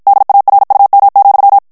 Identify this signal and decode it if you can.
Morse code, heard as a keyed tone, about 53 wpm, 800 Hz. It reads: DARUM,
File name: morse mp3 / ALLEMAND texte clair 2 vitesse 50 mots minute